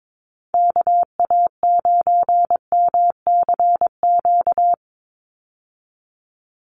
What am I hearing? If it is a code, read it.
Morse code, heard as a keyed tone, about 22 wpm, 700 Hz. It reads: KA9MCQ
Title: USA Call Signs 22wpm